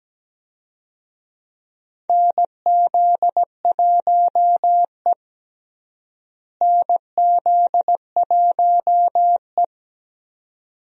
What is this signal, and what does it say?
Morse code, heard as a keyed tone, about 17 wpm, 700 Hz. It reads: NZ1E NZ1E
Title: USA Call Signs - 4 Characters - Repeated 2x - 17wpm